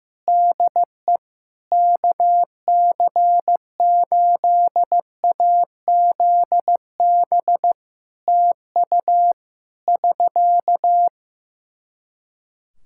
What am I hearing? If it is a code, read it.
Morse code, heard as a keyed tone, about 15 wpm, 700 Hz. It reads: DE KC8AZB TU <SK>
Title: Practice QSO - Exchange 4 - Ending - 15wpm